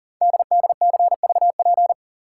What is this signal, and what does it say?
Morse code, heard as a keyed tone, about 39 wpm, 700 Hz. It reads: DDCVP
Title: Code Groups - 5 Letters - 40wpm